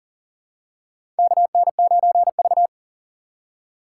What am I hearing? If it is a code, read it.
Morse code, heard as a keyed tone, about 37 wpm, 700 Hz. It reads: KN9V
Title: USA Call Signs - 4 Characters - 6x Character Spacing - 40wpm